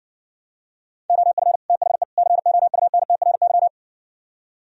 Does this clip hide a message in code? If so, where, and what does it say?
Yes: Morse code from 1.1 to 3.7 s, heard as a keyed tone, about 59 wpm, 700 Hz. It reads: OF THE COUNTRY